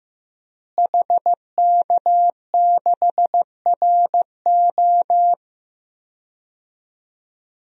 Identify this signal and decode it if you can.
Morse code, heard as a keyed tone, about 15 wpm, 700 Hz. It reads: HK6RO
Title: DXCC-100 Call Signs - 5 Characters - Repeated 3x - 15wpm